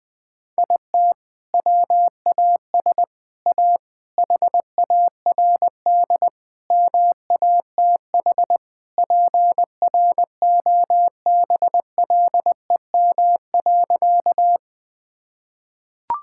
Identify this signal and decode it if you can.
Morse code, heard as a keyed tone, about 20 wpm, 700 Hz. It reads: IT WAS A HARD MATH PROBLEM.
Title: Sentences from Top 1000 Words - Review Part 6 - 20wpm